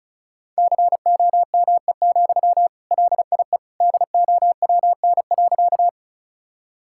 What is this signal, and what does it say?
Morse code, heard as a keyed tone, about 34 wpm, 700 Hz. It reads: COME, LIE DOWN.
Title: Sentences from Top 1600 Words 35wpm